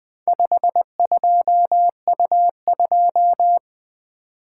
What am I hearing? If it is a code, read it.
Morse code, heard as a keyed tone, about 20 wpm, 700 Hz. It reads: 52U2
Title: Four Letter-Number Combinations - 20wpm